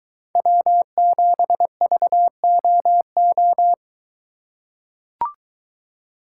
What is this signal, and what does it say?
Morse code, heard as a keyed tone, about 23 wpm, 700 Hz. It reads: W7VOO